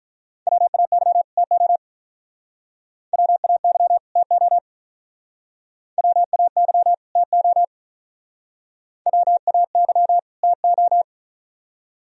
Morse code, heard as a keyed tone, about 47 wpm, 700 Hz. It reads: WAY TO WAY TO WAY TO WAY TO